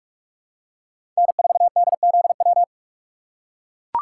Morse code, heard as a keyed tone, about 41 wpm, 700 Hz. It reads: N4DZW
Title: USA Call Signs - 5 Characters - 3x Character Spacing - 45wpm